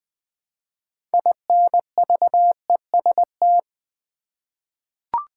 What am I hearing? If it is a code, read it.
Morse code, heard as a keyed tone, about 20 wpm, 700 Hz. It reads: INVEST
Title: Sentences from Top 1100 Words - New Word Drill Alt v2 - 20wpm